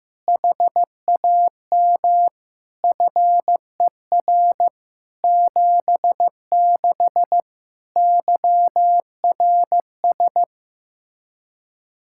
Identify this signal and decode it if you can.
Morse code, heard as a keyed tone, about 16 wpm, 700 Hz. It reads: HAM FER 76 YRS
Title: Practice QSO - Exchange 3 - More Info - 15wpm